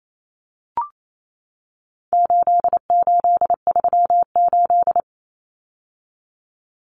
Morse code, heard as a keyed tone, about 28 wpm, 700 Hz. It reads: 8838
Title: Numbers - 4-Digit 28wpm